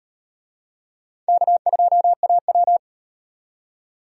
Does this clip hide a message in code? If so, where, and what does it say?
Yes: Morse code from 1.3 to 2.8 s, heard as a keyed tone, about 37 wpm, 700 Hz. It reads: K2AW